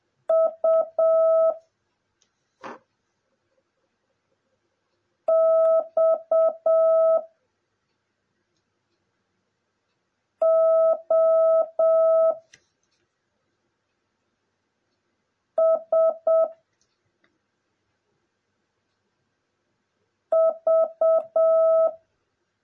Morse code, heard as a keyed tone, about 7 wpm, 650 Hz. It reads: U X O S V
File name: jh8cbh / 2021cw / cw09a